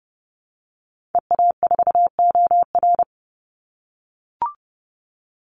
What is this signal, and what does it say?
Morse code, heard as a keyed tone, about 30 wpm, 700 Hz. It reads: EA4OR
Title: Call Signs - 30wpm